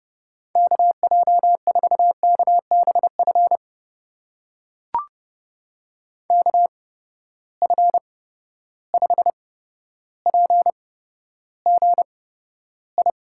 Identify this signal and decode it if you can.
Morse code, heard as a keyed tone, about 30 wpm, spaced as for 13 wpm, 700 Hz. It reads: KJ4KBF KF5PGI
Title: USA Call Signs - 6 Characters - 8x Character Spacing - 30wpm